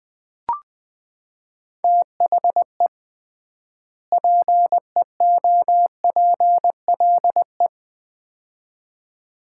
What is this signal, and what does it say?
Morse code, heard as a keyed tone, about 20 wpm, 700 Hz. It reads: THE PEOPLE
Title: 3x Word-Spacing - Sets of 2 Words - 20wpm